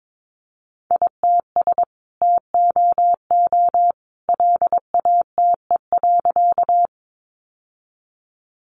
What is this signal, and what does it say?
Morse code, heard as a keyed tone, about 22 wpm, 700 Hz. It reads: ITS TOO LATE.
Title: Sentences from Top 600 Words 22wpm